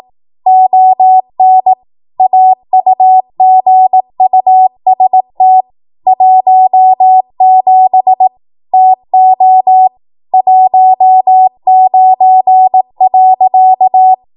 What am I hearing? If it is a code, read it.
Morse code, heard as a keyed tone, about 18 wpm, 750 Hz. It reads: ON AUGUST 17 TO 19.
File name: Morse / 18WPM